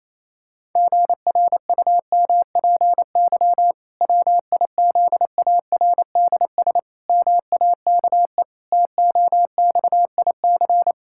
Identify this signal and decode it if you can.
Morse code, heard as a keyed tone, about 28 wpm, 700 Hz. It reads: GRUMPY WIZARDS MAKE TOXIC
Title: Pangram Challenge v2 - 28wpm